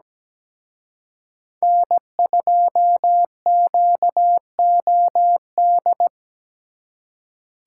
Morse code, heard as a keyed tone, about 20 wpm, 700 Hz. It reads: N2QOD
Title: Speed Racing - USA 1x3 Call Signs 17wpm